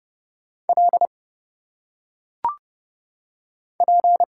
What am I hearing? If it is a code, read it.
Morse code, heard as a keyed tone, about 30 wpm, 700 Hz. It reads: L P
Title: Single Letters - Phonetic Pronunciation 30wpm